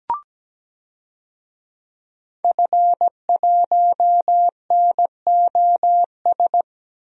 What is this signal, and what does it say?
Morse code, heard as a keyed tone, about 17 wpm, 700 Hz. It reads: F1NOS
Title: DXCC-100 Call Signs - 5 Characters - 17wpm